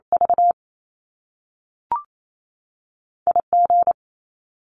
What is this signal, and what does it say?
Morse code, heard as a keyed tone, about 28 wpm, spaced as for 6 wpm, 700 Hz. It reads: VIG
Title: Top Two Letter Combinations in Words 28wpm